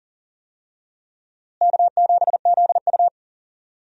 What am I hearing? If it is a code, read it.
Morse code, heard as a keyed tone, about 38 wpm, 700 Hz. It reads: K7ZU
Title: USA Call Signs - 4 Characters - Repeated 2x - 40wpm